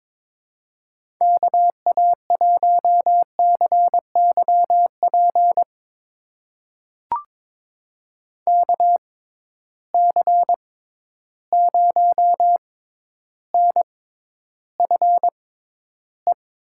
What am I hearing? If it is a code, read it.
Morse code, heard as a keyed tone, about 22 wpm, spaced as for 11 wpm, 700 Hz. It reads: KA1CYP KC0NFE
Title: USA Call Signs - 6 Characters - 6x Character Spacing - 22wpm